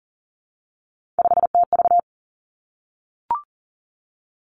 Morse code, heard as a keyed tone, about 41 wpm, 700 Hz. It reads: 5TV
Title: Three Challenging Letter-Number Combinations 40wpm